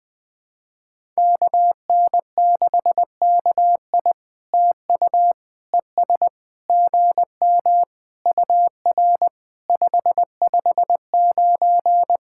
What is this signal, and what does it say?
Morse code, heard as a keyed tone, about 20 wpm, 700 Hz. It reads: KN6KI TU ES GM UR 559
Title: POTA QSO Practice - 20wpm